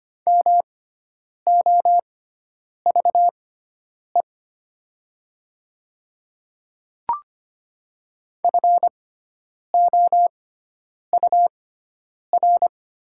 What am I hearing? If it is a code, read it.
Morse code, heard as a keyed tone, about 25 wpm, spaced as for 9 wpm, 700 Hz. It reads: MOVE FOUR
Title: 6x Character Spacing - Top 300 Words - 25wpm